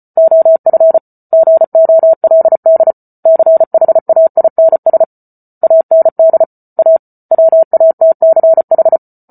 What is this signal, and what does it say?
Morse code, heard as a keyed tone, about 34 wpm, 650 Hz. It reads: OF GOLD CHAINS AND A WATCH